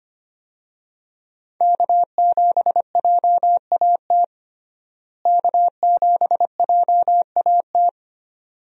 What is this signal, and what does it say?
Morse code, heard as a keyed tone, about 25 wpm, 700 Hz. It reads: K7JAT K7JAT
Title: USA Call Signs - 5 Characters - Repeated 2x - 25wpm